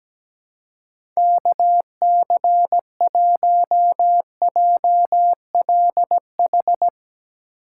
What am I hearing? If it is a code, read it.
Morse code, heard as a keyed tone, about 17 wpm, 700 Hz. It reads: KC1JLH
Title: USA Call Signs - 6 Characters - 2x Character Spacing - 17wpm